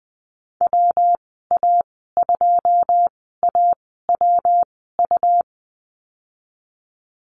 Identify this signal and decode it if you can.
Morse code, heard as a keyed tone, about 20 wpm, spaced as for 14 wpm, 700 Hz. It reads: WA2AWU
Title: 231 Lesson 37g - Letter x - Call Signs 20wpm